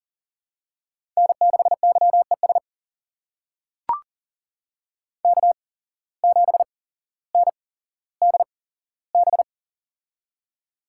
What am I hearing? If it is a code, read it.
Morse code, heard as a keyed tone, about 37 wpm, spaced as for 17 wpm, 700 Hz. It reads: N6YES K7NDB
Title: USA Call Signs - 8x Character Spacing - 40wpm